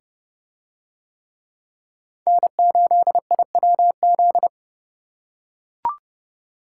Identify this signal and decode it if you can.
Morse code, heard as a keyed tone, about 27 wpm, 700 Hz. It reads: N8IWZ